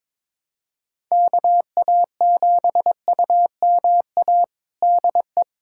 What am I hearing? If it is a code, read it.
Morse code, heard as a keyed tone, about 22 wpm, 700 Hz. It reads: KA7UMA DE
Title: USA Callsign DE Callsign - 6 Characters - 22wpm